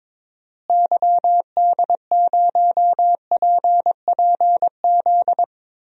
Morse code, heard as a keyed tone, about 22 wpm, 700 Hz. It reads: YD0PPZ
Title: DXCC-100 Call Signs - 6 Characters - Repeated 3x - 22wpm